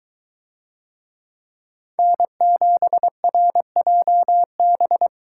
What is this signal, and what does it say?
Morse code, heard as a keyed tone, about 22 wpm, 700 Hz. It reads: N7RJB